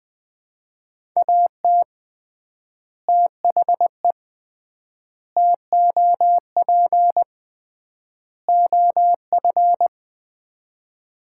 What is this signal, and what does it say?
Morse code, heard as a keyed tone, about 21 wpm, 700 Hz. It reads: AT THE TOP OF
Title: 3x Word-Spacing - Sets of 4 Words - 20wpm